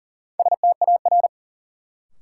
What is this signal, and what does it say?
Morse code, heard as a keyed tone, about 40 wpm, 700 Hz. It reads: STAR
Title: Sentences from Top 1000 Words - Review Part 2 - New Word Drill - 40wpm